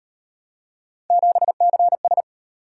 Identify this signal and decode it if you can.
Morse code, heard as a keyed tone, about 35 wpm, 700 Hz. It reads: 7CS